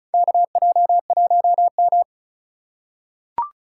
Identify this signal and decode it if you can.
Morse code, heard as a keyed tone, about 36 wpm, 700 Hz. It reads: KJ1M
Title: USA Call Signs - 4 Characters - 2x Character Spacing - 35wpm